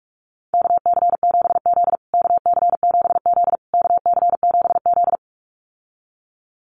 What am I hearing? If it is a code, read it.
Morse code, heard as a keyed tone, about 44 wpm, 700 Hz. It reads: KC7Z KC7Z KC7Z
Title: USA Call Signs Repeated 3x 45wpm